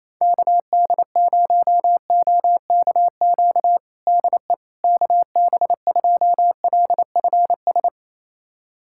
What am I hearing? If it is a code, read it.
Morse code, heard as a keyed tone, about 28 wpm, 700 Hz. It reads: KD0OKQ DE KB2LFS
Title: USA Callsign DE Callsign - 6 Characters - 28wpm